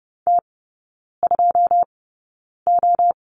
Morse code, heard as a keyed tone, about 30 wpm, spaced as for 9 wpm, 700 Hz. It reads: T2O
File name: 081 Lesson 14e - Number 2 - Three Characters 30wpm